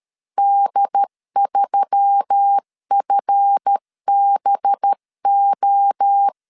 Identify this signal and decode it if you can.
Morse code, heard as a keyed tone, about 13 wpm, 800 Hz. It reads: D3FBO
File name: LEZIONI / 7 GF9Z / 7 GF 12wpm 2122